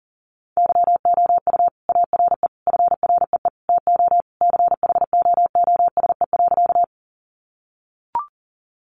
Morse code, heard as a keyed tone, about 40 wpm, 700 Hz. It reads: YOU ARE FREE TO CHOOSE.
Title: Sentences from Top 700 Words 40wpm